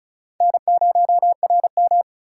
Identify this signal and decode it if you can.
Morse code, heard as a keyed tone, about 34 wpm, 700 Hz. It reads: N0RM
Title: USA Call Signs - 4x Character Spacing - 35wpm